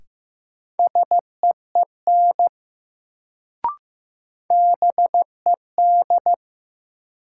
Morse code, heard as a keyed tone, about 15 wpm, 700 Hz. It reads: SEEN BED
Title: Sets of 5 Words Encore - New Word Drill - 15wpm